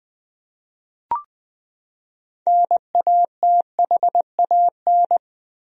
Morse code, heard as a keyed tone, about 20 wpm, 700 Hz. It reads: NATHAN
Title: USA Common Names - 20wpm